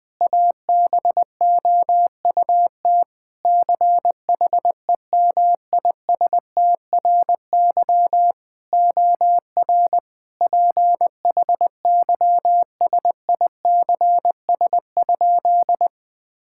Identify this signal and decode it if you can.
Morse code, heard as a keyed tone, about 21 wpm, 700 Hz. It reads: ABOUT CHEMISTRY OR PHYSICS?
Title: Sentences from Top 1000 Words - Review Part 6 - 20wpm